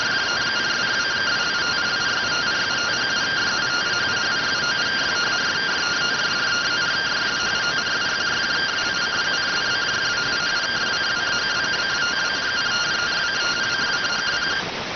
Two-tone teleprinter signal, about 170 Hz shift.